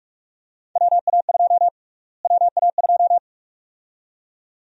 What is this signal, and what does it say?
Morse code, heard as a keyed tone, about 41 wpm, 700 Hz. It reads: WA2 WA2
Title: USA Call Sign Prefixes - Repeated 2x - 45wpm